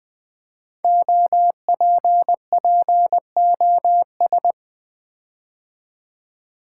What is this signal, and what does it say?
Morse code, heard as a keyed tone, about 20 wpm, 700 Hz. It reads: OPPOS
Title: Build A Word - 5 Letters At A Time - 10-Letter Words - 20wpm